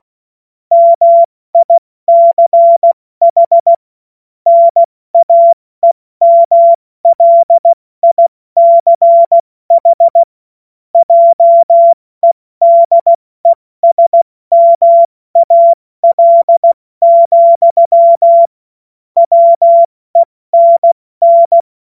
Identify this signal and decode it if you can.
Morse code, heard as a keyed tone, about 16 wpm, spaced as for 14 wpm, 680 Hz. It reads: MICH NAEMLICH JEDESMAL, WENN
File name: Morsekurs / Witze / Witz 06 Februar 2024 70 F80